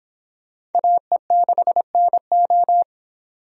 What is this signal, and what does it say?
Morse code, heard as a keyed tone, about 26 wpm, 700 Hz. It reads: AE6NO